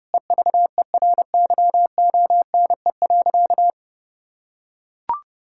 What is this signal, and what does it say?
Morse code, heard as a keyed tone, about 30 wpm, 700 Hz. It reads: EVERYONE.
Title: Sentences from Top 1000 Words - Review Part 5 - 30wpm